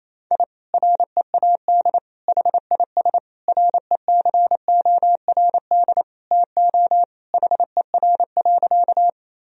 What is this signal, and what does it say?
Morse code, heard as a keyed tone, about 27 wpm, 700 Hz. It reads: I READ HIS RECORD TO HER.
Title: Sentences from Top 1000 Words - Review Part 3 - 28wpm